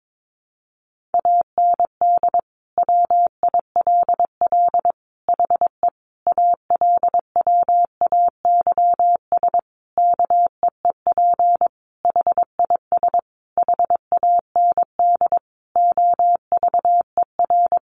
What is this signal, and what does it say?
Morse code, heard as a keyed tone, about 22 wpm, 700 Hz. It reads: AND WILL HE ALWAYS KEEP HIS HAND OVER
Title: Sentences from Top 1200 Words 22wpm